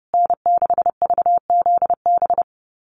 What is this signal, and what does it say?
Morse code, heard as a keyed tone, about 30 wpm, 700 Hz. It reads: N6VZB
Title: USA Call Signs Repeated 3x 30wpm